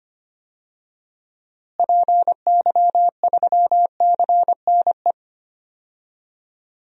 Morse code, heard as a keyed tone, about 25 wpm, 700 Hz. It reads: PY3CNE
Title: DXCC-100 Call Signs - 6 Characters - Repeated 3x - 25wpm